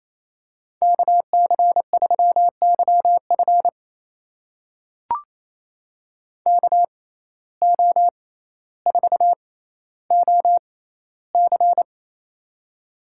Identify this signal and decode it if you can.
Morse code, heard as a keyed tone, about 28 wpm, spaced as for 14 wpm, 700 Hz. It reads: KC3YF KO4OC
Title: USA Call Signs - 5 Characters - 6x Character Spacing - 28wpm